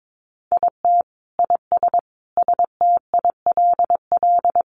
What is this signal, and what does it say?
Morse code, heard as a keyed tone, about 22 wpm, 700 Hz. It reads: IT IS STILL
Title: Easy Phrases 22wpm